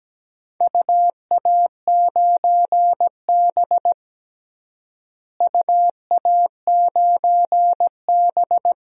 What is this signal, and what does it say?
Morse code, heard as a keyed tone, about 17 wpm, 700 Hz. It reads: UA9B UA9B
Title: DXCC-100 Call Signs - 4 Characters - Repeated 2x - 17wpm